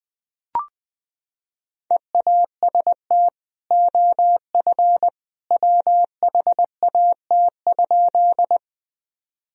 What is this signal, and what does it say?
Morse code, heard as a keyed tone, about 21 wpm, 700 Hz. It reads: EAST OF WHAT?
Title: Sentences from Top 1000 Words - Review Part 3 - 20wpm